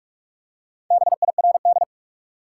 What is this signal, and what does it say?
Morse code, heard as a keyed tone, about 46 wpm, 700 Hz. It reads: BIRD
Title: Four Letter Words - 45wpm